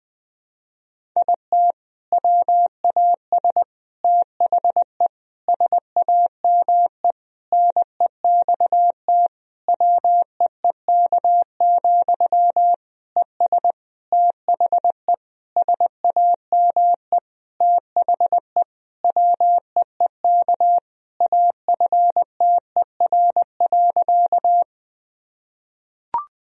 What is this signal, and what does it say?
Morse code, heard as a keyed tone, about 20 wpm, 700 Hz. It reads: IT WAS THE SAME NEXT WEEK, ES THE SAME THE WEEK AFTER.
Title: Sentences from Top 500 Words and Abbreviations 20wpm